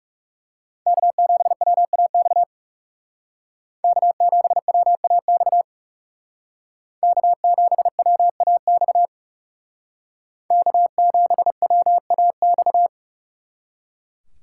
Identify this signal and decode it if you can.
Morse code, heard as a keyed tone, about 38 wpm, 700 Hz. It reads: K7WAX K7WAX K7WAX K7WAX